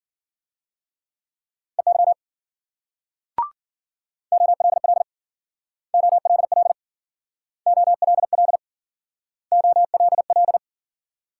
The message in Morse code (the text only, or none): EY OLL OLL OLL OLL